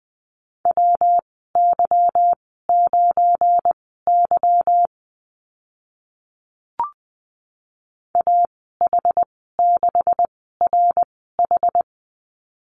Morse code, heard as a keyed tone, about 20 wpm, spaced as for 15 wpm, 700 Hz. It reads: WY9Y AH6RH
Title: 186 Lesson 30g - Letter b - Call Signs 20wpm